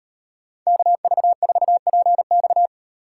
Morse code, heard as a keyed tone, about 35 wpm, 700 Hz. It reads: KV4PX